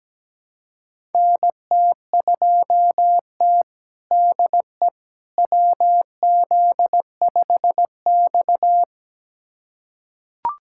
Morse code, heard as a keyed tone, about 17 wpm, 700 Hz. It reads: NT2T DE WZ5X
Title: USA Callsign DE Callsign - 4 Characters - 17wpm